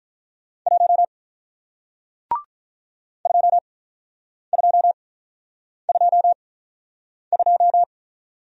Morse code, heard as a keyed tone, about 48 wpm, spaced as for 9 wpm, 700 Hz. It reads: 1 2222